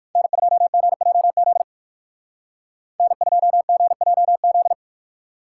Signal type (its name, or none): Morse code